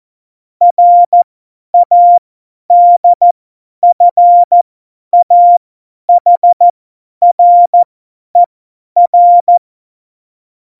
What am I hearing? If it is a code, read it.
Morse code, heard as a keyed tone, about 14 wpm, spaced as for 10 wpm, 700 Hz. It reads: RADFAHRER
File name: Morsekurs / Witze / Witz 05. Februar 2019, 50 F70